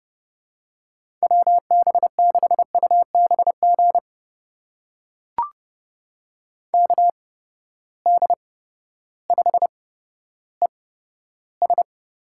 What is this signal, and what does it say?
Morse code, heard as a keyed tone, about 30 wpm, spaced as for 14 wpm, 700 Hz. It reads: WB6UBG KD5ES
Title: USA Call Signs - 6 Characters - 8x Character Spacing - 30wpm